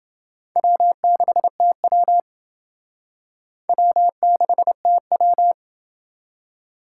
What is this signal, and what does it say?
Morse code, heard as a keyed tone, about 29 wpm, 700 Hz. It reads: W6TW W6TW